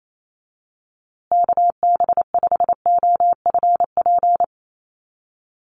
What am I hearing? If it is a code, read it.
Morse code, heard as a keyed tone, about 28 wpm, 700 Hz. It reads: KB5OFP
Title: USA Call Signs Repeated 2x 28wpm